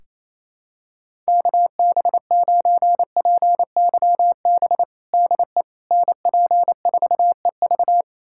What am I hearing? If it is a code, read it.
Morse code, heard as a keyed tone, about 28 wpm, 700 Hz. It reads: KB9PYB DE NP4EV
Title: USA Callsign DE Callsign 28wpm